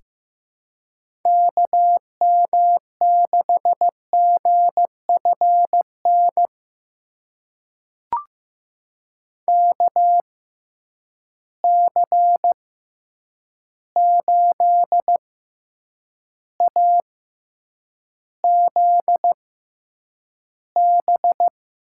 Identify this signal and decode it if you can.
Morse code, heard as a keyed tone, about 15 wpm, spaced as for 8 wpm, 700 Hz. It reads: KM6GFN KC8AZB
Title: USA Call Signs - 6 Characters - 6x Character Spacing - 15wpm